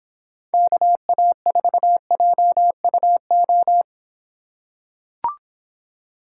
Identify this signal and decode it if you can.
Morse code, heard as a keyed tone, about 26 wpm, 700 Hz. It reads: KA4JUO